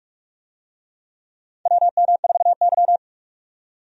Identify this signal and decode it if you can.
Morse code, heard as a keyed tone, about 45 wpm, 700 Hz. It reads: WM4Y
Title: USA Call Signs - 4 Characters - 45wpm